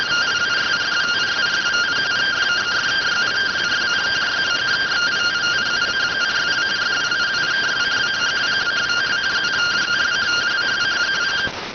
Two-tone teleprinter signal, about 170 Hz shift.